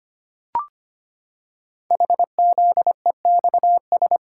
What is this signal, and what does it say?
Morse code, heard as a keyed tone, about 25 wpm, 700 Hz. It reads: HZEXS
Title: Code Groups - 5 Letters - 25wpm